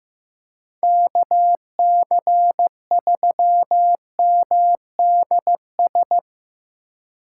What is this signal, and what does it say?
Morse code, heard as a keyed tone, about 17 wpm, 700 Hz. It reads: KC3MDS
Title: Speed Racing - USA 2x3 Call Signs 15wpm